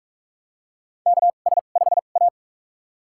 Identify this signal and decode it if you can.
Morse code, heard as a keyed tone, about 44 wpm, spaced as for 32 wpm, 690 Hz. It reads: KS5A
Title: USA Call Signs - 4 Characters - 2x Character Spacing - 45wpm